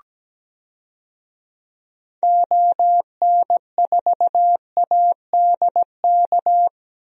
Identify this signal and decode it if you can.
Morse code, heard as a keyed tone, about 17 wpm, 700 Hz. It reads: ON4ADK
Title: DXCC-100 Call Signs - 6 Characters - Repeated 3x - 17wpm